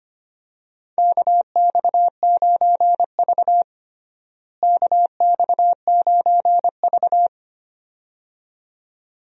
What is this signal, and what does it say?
Morse code, heard as a keyed tone, about 25 wpm, 700 Hz. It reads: KX9V KX9V
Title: USA Call Signs - 4 Characters - Repeated 2x - 25wpm